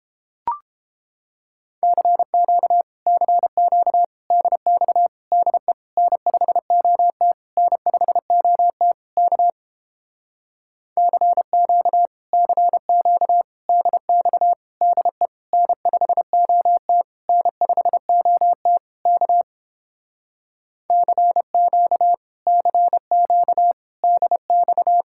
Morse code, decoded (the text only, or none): CQ CQ DX DE N5OT N5OT K CQ CQ DX DE N5OT N5OT K CQ CQ DX